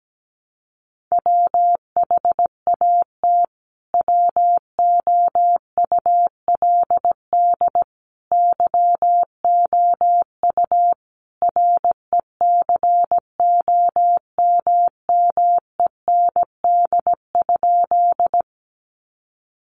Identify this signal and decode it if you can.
Morse code, heard as a keyed tone, about 17 wpm, 700 Hz. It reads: WHAT WOULD YOU RECOMMEND?
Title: Sentences from Top 1200 Words 17wpm